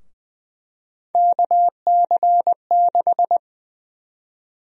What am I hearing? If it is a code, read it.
Morse code, heard as a keyed tone, about 20 wpm, 700 Hz. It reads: KC6
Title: USA Call Sign Prefixes - 20wpm